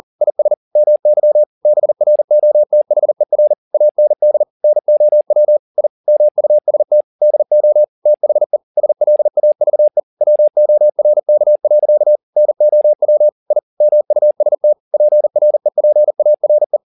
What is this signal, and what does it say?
Morse code, heard as a keyed tone, about 40 wpm, 600 Hz. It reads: IS MY BROTHER AND NOW I MUST DO THE SLAVE WORK. NOW I MUST PREPARE
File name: the old man and the sea 2 40wpm